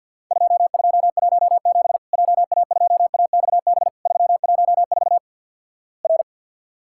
Morse code, heard as a keyed tone, about 50 wpm, 700 Hz. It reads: 2217 JA2AXB 314 R
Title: Worked All Europe DX Contest - QTC Exchange - 50wpm